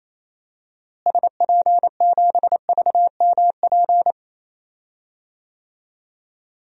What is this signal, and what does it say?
Morse code, heard as a keyed tone, about 28 wpm, 700 Hz. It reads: SP7VMP
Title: DXCC-100 Call Signs - 6 Characters - Repeated 3x - 28wpm